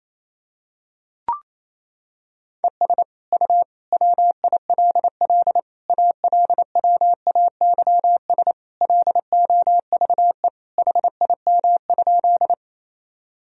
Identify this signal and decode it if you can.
Morse code, heard as a keyed tone, about 28 wpm, 700 Hz. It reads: ES U WILL ALWAYS LOVE HIM?
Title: Sentences from Top 400 Words and Abbreviations 28wpm